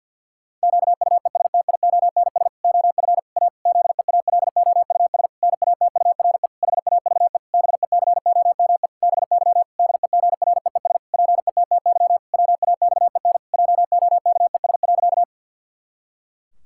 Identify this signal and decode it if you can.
Morse code, heard as a keyed tone, about 50 wpm, 700 Hz. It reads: QUESTIONS OF A ZEALOUS NATURE HAVE BECOME BY DEGREES PETTY WAXEN JOKES.
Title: Pangram Challenge v2 - 50wpm